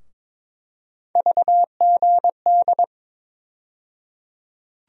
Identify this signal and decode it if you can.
Morse code, heard as a keyed tone, about 22 wpm, 700 Hz. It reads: VGD
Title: USA Call Sign Postfixes - 3 Letters 22wpm